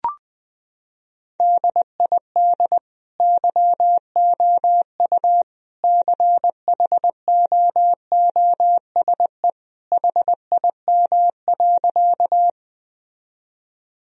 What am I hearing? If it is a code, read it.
Morse code, heard as a keyed tone, about 21 wpm, 700 Hz. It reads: DID YOU CHOOSE HIM.
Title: Sentences from Top 1000 Words - Review Part 4 - 20wpm